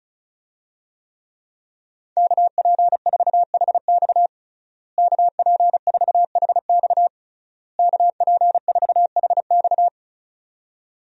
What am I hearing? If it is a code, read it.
Morse code, heard as a keyed tone, about 36 wpm, 700 Hz. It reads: KP4HX KP4HX KP4HX
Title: USA Call Signs - 5 Characters - Repeated 3x - 35wpm